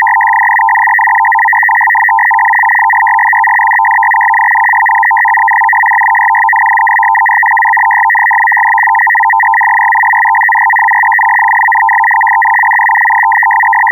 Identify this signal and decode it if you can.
Two-tone teleprinter signal, about 850 Hz shift.